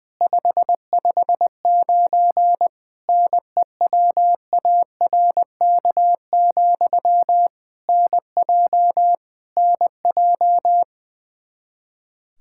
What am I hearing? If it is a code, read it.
Morse code, heard as a keyed tone, about 21 wpm, 700 Hz. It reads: 559 NEWARK, NJ NJ
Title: Practice QSO - Exchange 1 - Beginning - 20wpm